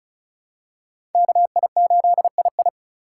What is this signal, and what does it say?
Morse code, heard as a keyed tone, about 36 wpm, 700 Hz. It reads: KI8II
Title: USA Call Signs - 3x Character Spacing - 35wpm